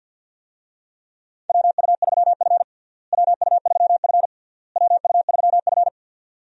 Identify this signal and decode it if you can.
Morse code, heard as a keyed tone, about 52 wpm, 700 Hz. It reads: WU3F WU3F WU3F
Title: USA Call Signs - 4 Characters - Repeated 3x - 50wpm